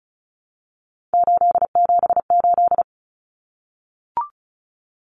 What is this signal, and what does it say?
Morse code, heard as a keyed tone, about 34 wpm, 700 Hz. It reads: 878
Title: Common Cut Numbers - 3-Digit 35wpm